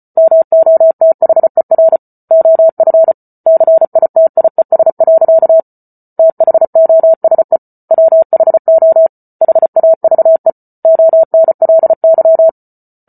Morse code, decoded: MOTHER OF CITIES. THOSE WHO HAVE ONLY

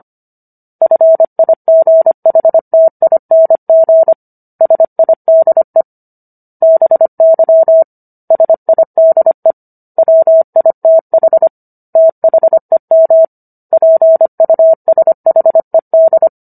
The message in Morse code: FIGHTING SIDE BY SIDE WITH THEM PUSHED

